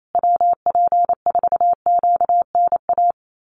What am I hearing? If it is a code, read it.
Morse code, heard as a keyed tone, about 28 wpm, 700 Hz. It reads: WP4QNA
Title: USA Call Signs Repeated 3x 28wpm